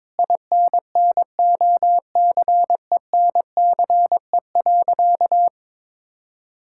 Morse code, heard as a keyed tone, about 22 wpm, 700 Hz. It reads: INNOCENCE.
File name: Sentences with 7-Word Phrases - 22wpm